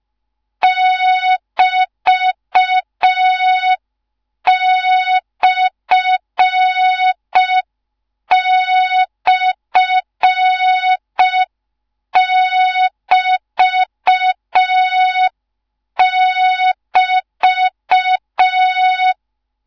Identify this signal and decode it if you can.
Morse code, heard as a keyed tone, about 5 wpm, 3700 Hz. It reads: <BT>//<BT><BT>